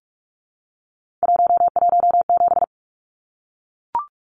Morse code, heard as a keyed tone, about 44 wpm, 700 Hz. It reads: 117
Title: Numbers - 3-Digit 45wpm